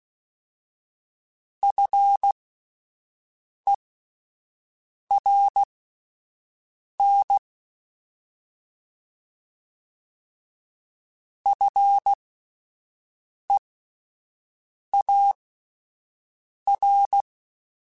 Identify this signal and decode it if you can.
Morse code, heard as a keyed tone, about 16 wpm, spaced as for 5 wpm, 800 Hz. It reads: FERN FEAR
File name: Alphabetical / Alphabetical mp3s / Fs @5